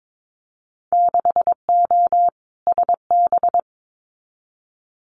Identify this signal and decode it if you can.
Morse code, heard as a keyed tone, about 22 wpm, 700 Hz. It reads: 6O SB
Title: Winter Field Day Exchanges Encore 22wpm